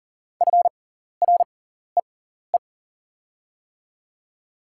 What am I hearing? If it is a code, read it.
Morse code, heard as a keyed tone, about 41 wpm, spaced as for 14 wpm, 700 Hz. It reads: FREE
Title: 6x Character Spacing - Top 500 Words - 40wpm